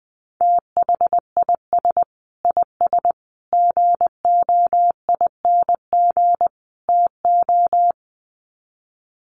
Morse code, heard as a keyed tone, about 20 wpm, 700 Hz. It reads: THIS IS GOING TO
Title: Sets of 4 Words 20wpm